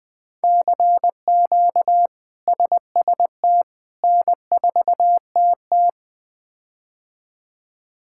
Morse code, heard as a keyed tone, about 20 wpm, 700 Hz. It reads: CQ SST N4TT
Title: K1USN SST QSO Practice - 20wpm